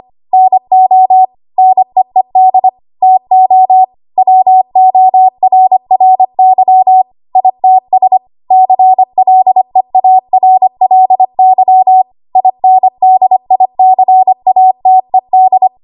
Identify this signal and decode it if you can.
Morse code, heard as a keyed tone, about 25 wpm, 750 Hz. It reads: NO NEED TO WORRY ITS CLEARLY INDICATED
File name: Archive / 25 WPM / 25WPM